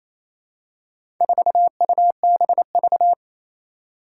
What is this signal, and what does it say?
Morse code, heard as a keyed tone, about 28 wpm, 700 Hz. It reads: 4UBV
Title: Four Challenging Letter-Number Combinations 28wpm